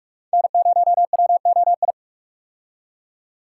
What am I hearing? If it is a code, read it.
Morse code, heard as a keyed tone, about 42 wpm, 700 Hz. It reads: N0WOI
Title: K1USN SST QSO Practice - 45wpm